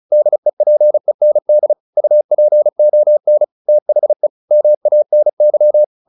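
Morse code, heard as a keyed tone, about 34 wpm, 600 Hz. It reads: DEPEND UPON THE MANY
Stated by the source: Morse practice at 35 wpm from CW med Gnister, sending Ken Follett's novel Never in English.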